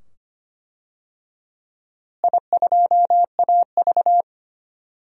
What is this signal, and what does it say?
Morse code, heard as a keyed tone, about 25 wpm, 700 Hz. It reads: I2AV
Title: DXCC-100 Call Signs - 4 Characters - Repeated 2x - 25wpm